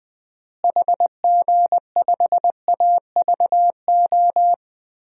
Morse code, heard as a keyed tone, about 20 wpm, 700 Hz. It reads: HG5AVO
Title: DXCC-100 Call Signs - 6 Characters - Repeated 2x - 20wpm